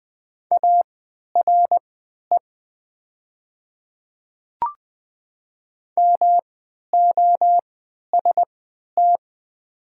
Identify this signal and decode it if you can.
Morse code, heard as a keyed tone, about 20 wpm, spaced as for 11 wpm, 700 Hz. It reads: ARE MOST
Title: 3x Character Spacing - Top 100 Words - 20wpm